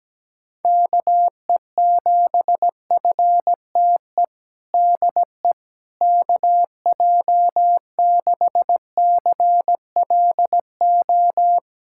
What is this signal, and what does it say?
Morse code, heard as a keyed tone, about 17 wpm, 700 Hz. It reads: KE7FTE DE KJ6CLO
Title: USA Callsign DE Callsign - 6 Characters - 17wpm